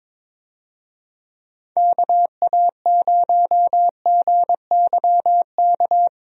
Morse code, heard as a keyed tone, about 22 wpm, 700 Hz. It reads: KA0GYK
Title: USA Call Signs - 6 Characters - Repeated 2x - 22wpm